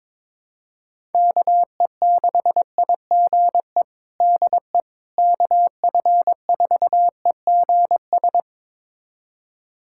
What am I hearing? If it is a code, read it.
Morse code, heard as a keyed tone, about 22 wpm, 700 Hz. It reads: KE6IGE DE KF4EGS
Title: USA Callsign DE Callsign - 6 Characters - 22wpm